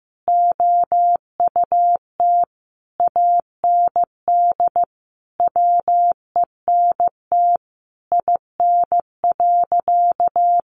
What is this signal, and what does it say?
Morse code, heard as a keyed tone, about 15 wpm, 700 Hz. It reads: OUT AND WENT IN.
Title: Sentences from Top 300 Words 15wpm